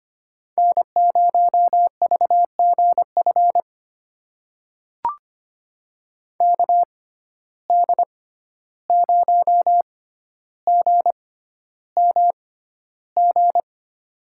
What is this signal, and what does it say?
Morse code, heard as a keyed tone, about 25 wpm, spaced as for 12 wpm, 700 Hz. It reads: N0VGF KD0GMG
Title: USA Call Signs - 6x Character Spacing - 25wpm